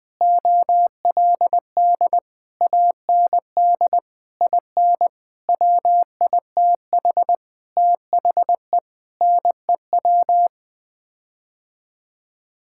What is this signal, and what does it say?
Morse code, heard as a keyed tone, about 20 wpm, 700 Hz. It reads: OLD AND IN WITH THE NEW
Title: Sets of 9 Words - 20wpm